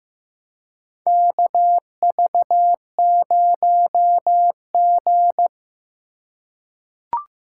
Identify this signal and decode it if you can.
Morse code, heard as a keyed tone, about 15 wpm, 700 Hz. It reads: KV0G
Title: USA Call Signs - 4 Characters - 2x Character Spacing - 15wpm